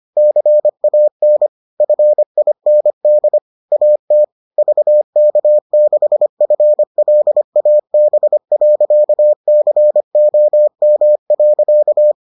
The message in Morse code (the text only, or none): CAN FIND AT VK6FLAB.COM.